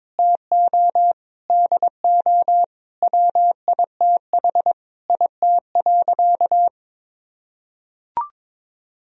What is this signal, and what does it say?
Morse code, heard as a keyed tone, about 22 wpm, 700 Hz. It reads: TO DO WITH IT.
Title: Sentences from Top 1000 Words - Review Part 1 - 22wpm